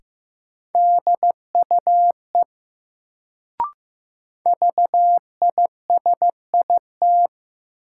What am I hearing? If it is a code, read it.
Morse code, heard as a keyed tone, about 15 wpm, 700 Hz. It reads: DUE VISIT
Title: Sentences from Top 900 Words - New Word Drill - 15wpm